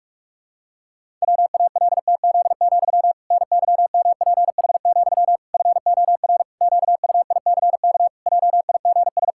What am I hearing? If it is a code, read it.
Morse code, heard as a keyed tone, about 46 wpm, 700 Hz. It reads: WALTZ, NYMPH, FOR QUICK JIGS
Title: Pangram Challenge v2 - 45wpm